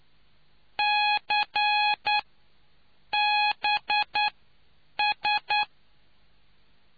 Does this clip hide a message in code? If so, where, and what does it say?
Yes: Morse code from 0.8 to 5.6 s, heard as a keyed tone, about 9 wpm, spaced as for 7 wpm, 840 Hz. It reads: CBS